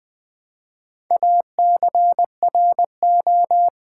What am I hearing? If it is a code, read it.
Morse code, heard as a keyed tone, about 20 wpm, 700 Hz. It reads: ACRO
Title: Build A Word - 2 Letters At A Time - 6-Letter Words - 20wpm